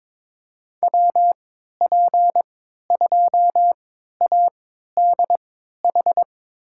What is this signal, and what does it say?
Morse code, heard as a keyed tone, about 22 wpm, spaced as for 13 wpm, 700 Hz. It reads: WP2ADH
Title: USA Call Signs - 6 Characters - 3x Character Spacing - 22wpm